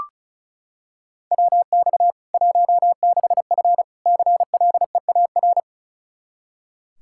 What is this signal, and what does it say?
Morse code, heard as a keyed tone, about 36 wpm, 700 Hz. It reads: WX 16F CLEAR
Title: Practice QSO - Exchange 2 - Info - 35wpm